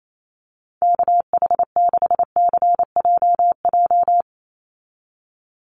Morse code, heard as a keyed tone, about 28 wpm, 700 Hz. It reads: KH6CJJ
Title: Call Signs Repeated 2x - 28wpm